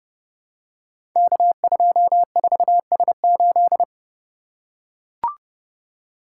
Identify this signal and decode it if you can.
Morse code, heard as a keyed tone, about 30 wpm, 700 Hz. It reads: K24S8
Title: Code Groups - 5 Letter:Numbers - 30wpm